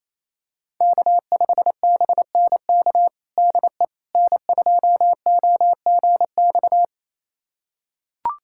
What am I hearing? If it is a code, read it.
Morse code, heard as a keyed tone, about 28 wpm, 700 Hz. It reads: K5BNK DE N2OGX
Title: USA Callsign DE Callsign - 5 Characters - 28wpm